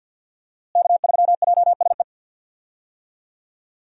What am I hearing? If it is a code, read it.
Morse code, heard as a keyed tone, about 49 wpm, 700 Hz. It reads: K3JSE
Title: USA Call Signs - 5 Characters - Repeated 3x - 50wpm